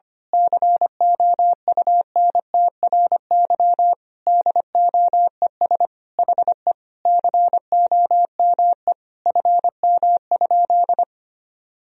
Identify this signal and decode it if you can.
Morse code, heard as a keyed tone, about 25 wpm, 700 Hz. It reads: COUNTRY DOES HE COME FM?
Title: Sentences from Top 300 Words and Abbreviations 25wpm